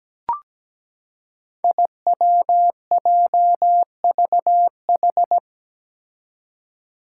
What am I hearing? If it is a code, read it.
Morse code, heard as a keyed tone, about 17 wpm, 700 Hz. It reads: IWJVH
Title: Five Challenging Letter-Number Combinations 17wpm